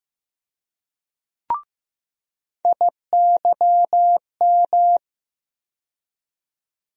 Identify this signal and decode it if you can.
Morse code, heard as a keyed tone, about 15 wpm, 700 Hz. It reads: IYM
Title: USA Call Sign Postfixes - 3 Letters 15wpm